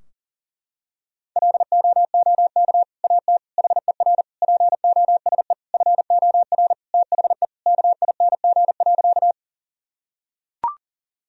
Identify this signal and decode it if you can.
Morse code, heard as a keyed tone, about 40 wpm, 700 Hz. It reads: LOOK AT HER POSE FOR THE KING.
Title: Sentences from Top 1000 Words - Review Part 2 - 40wpm